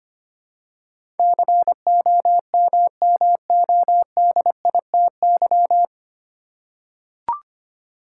Morse code, heard as a keyed tone, about 25 wpm, 700 Hz. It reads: COMMODITY
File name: Nine Letter Words - 25wpm